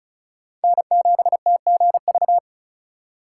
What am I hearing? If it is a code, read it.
Morse code, heard as a keyed tone, about 36 wpm, 700 Hz. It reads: N7TGV
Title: USA Call Signs - 4x Character Spacing - 35wpm